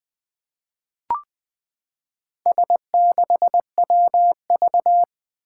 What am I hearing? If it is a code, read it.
Morse code, heard as a keyed tone, about 21 wpm, 700 Hz. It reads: S6WV
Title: Four Challenging Letter-Number Combinations 20wpm